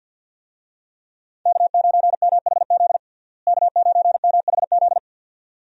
Morse code, heard as a keyed tone, about 50 wpm, 700 Hz. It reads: K9MHZ K9MHZ
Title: USA Call Signs - 5 Characters - Repeated 2x - 50wpm